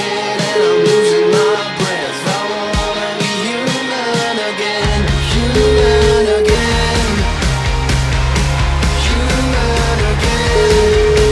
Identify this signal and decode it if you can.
Ringback tone (Europe, 425 Hz), with noise.